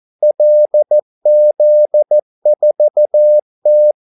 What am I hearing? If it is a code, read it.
Morse code, heard as a keyed tone, about 14 wpm, 600 Hz. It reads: LZ4T
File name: Callsigns-14wpm